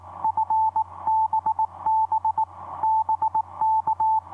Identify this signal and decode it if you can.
Morse code, heard as a keyed tone, about 23 wpm, spaced as for 20 wpm, 880 Hz. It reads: FBB BK
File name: h74leon ei9fbb etc 30m cw 10nov2010 2303